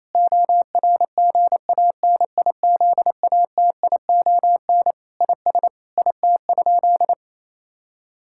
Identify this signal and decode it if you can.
Morse code, heard as a keyed tone, about 28 wpm, 700 Hz. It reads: ORGANIZATION IS IT?
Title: Sentences from Top 1000 Words - Review Part 5 - 28wpm